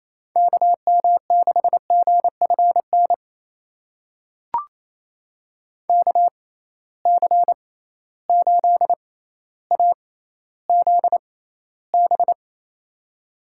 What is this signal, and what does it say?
Morse code, heard as a keyed tone, about 28 wpm, spaced as for 14 wpm, 700 Hz. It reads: KM6GFN KC8AZB
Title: USA Call Signs - 6 Characters - 6x Character Spacing - 28wpm